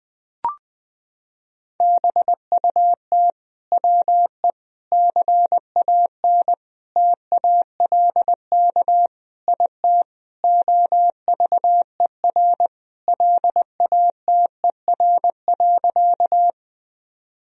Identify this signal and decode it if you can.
Morse code, heard as a keyed tone, about 20 wpm, 700 Hz. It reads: BUT WE CAN TALK IT OVER LATER.
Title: Sentences from Top 1000 Words - Review Part 4 - 20wpm